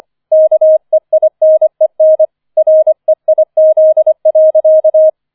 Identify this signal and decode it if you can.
Morse code, heard as a keyed tone, about 25 wpm, 630 Hz. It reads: KEINEN REIZ.